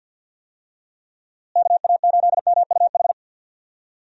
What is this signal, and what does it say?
Morse code, heard as a keyed tone, about 50 wpm, 700 Hz. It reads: KA8MUH
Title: USA Call Signs - 6 Characters - Repeated 3x - 50wpm